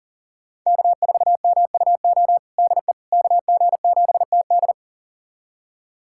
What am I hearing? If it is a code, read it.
Morse code, heard as a keyed tone, about 40 wpm, 700 Hz. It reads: K4MUO DE KG7TD
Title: USA Callsign DE Callsign - 5 Characters - 40wpm